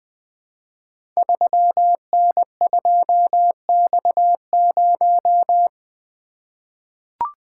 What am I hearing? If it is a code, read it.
Morse code, heard as a keyed tone, about 21 wpm, 700 Hz. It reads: 3N2X0
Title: Code Groups - 5 Letter:Numbers - 20wpm